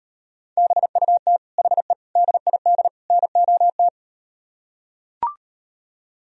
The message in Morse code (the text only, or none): BUT HE DID NOT